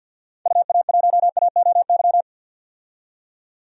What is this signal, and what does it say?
Morse code, heard as a keyed tone, about 48 wpm, 700 Hz. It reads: UA1AOY
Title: DXCC-100 Call Signs - 6 Characters - 50wpm